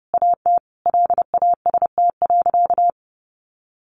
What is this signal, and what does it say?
Morse code, heard as a keyed tone, about 30 wpm, 700 Hz. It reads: AT LAST.
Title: Sentences from Top 1100 Words 30wpm